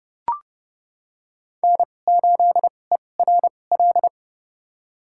Morse code, heard as a keyed tone, about 30 wpm, spaced as for 22 wpm, 700 Hz. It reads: N8ERL
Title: USA Call Signs - 5 Characters - 2x Character Spacing - 30wpm